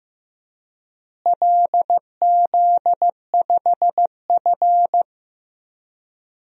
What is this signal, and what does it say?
Morse code, heard as a keyed tone, about 15 wpm, 700 Hz. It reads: LZ5F
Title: DXCC-100 Call Signs - 4 Characters - Repeated 3x - 15wpm